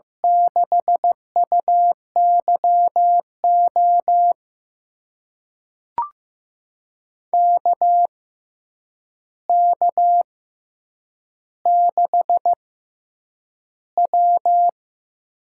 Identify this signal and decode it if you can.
Morse code, heard as a keyed tone, about 15 wpm, spaced as for 8 wpm, 700 Hz. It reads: 6UYO KK6W
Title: USA Call Signs - 6 Characters - 6x Character Spacing - 15wpm